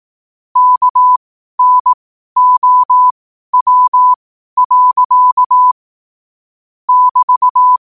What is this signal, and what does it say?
Morse code, heard as a keyed tone, about 18 wpm, spaced as for 12 wpm, 1000 Hz. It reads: KNOW. <BT>
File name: qotdfiles / 12wpm